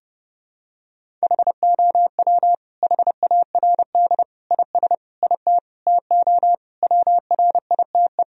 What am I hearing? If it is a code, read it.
Morse code, heard as a keyed tone, about 30 wpm, 700 Hz. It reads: HOW HARD IS IT TO WRITE
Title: Sentences from Top 1000 Words - Review Part 3 - 30wpm